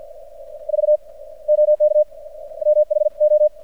Morse code, heard as a keyed tone, about 50 wpm, 600 Hz. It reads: V OK MH O
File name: archived / CRSnet